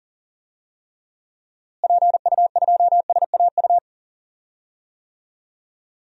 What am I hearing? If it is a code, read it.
Morse code, heard as a keyed tone, about 40 wpm, 700 Hz. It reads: PU2SAU
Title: DXCC-100 Call Signs - 6 Characters - Repeated 2x - 40wpm